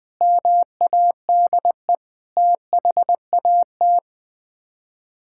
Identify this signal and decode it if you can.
Morse code, heard as a keyed tone, about 20 wpm, 700 Hz. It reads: MADE THAT
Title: Sets of 8 Words - 20wpm